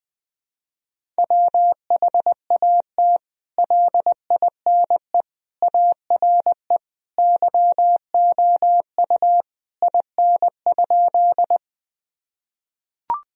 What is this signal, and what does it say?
Morse code, heard as a keyed tone, about 20 wpm, 700 Hz. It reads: WHAT LINE ARE YOU IN?
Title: Sentences from Top 1000 Words - Review Part 1 - 20wpm